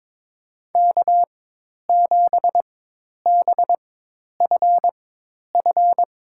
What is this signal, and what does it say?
Morse code, heard as a keyed tone, about 22 wpm, spaced as for 10 wpm, 700 Hz. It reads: K7BFF
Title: USA Call Signs - 5 Characters - 4x Character Spacing - 22wpm